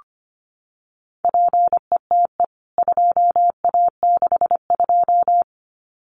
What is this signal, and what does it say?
Morse code, heard as a keyed tone, about 25 wpm, 700 Hz. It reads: PETE 2A62
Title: CWT Exchange 25wpm